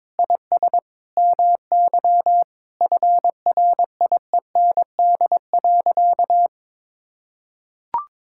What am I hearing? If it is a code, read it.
Morse code, heard as a keyed tone, about 22 wpm, 700 Hz. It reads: IS MY FRIEND.
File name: practice / 22wpm / Sentences with 8-Word Phrases - 22wpm